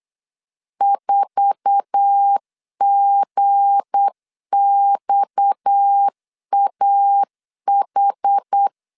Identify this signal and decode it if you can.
Morse code, heard as a keyed tone, about 8 wpm, 800 Hz. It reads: 4GXAH
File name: CW / DET / 1 9 1500C S B 8wpm 5624